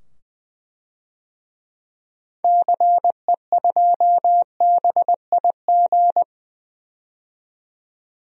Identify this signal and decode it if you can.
Morse code, heard as a keyed tone, about 20 wpm, 700 Hz. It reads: CE2BIG
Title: DXCC-100 Call Signs - 6 Characters - 20wpm